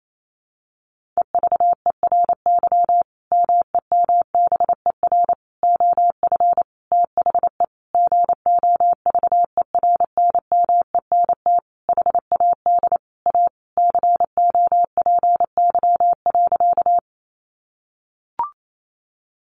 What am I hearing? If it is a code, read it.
Morse code, heard as a keyed tone, about 28 wpm, 700 Hz. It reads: EVERY MEMBER OF THE GOVERNMENT HAD A COPY.
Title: Sentences from Top 1400 Words 28wpm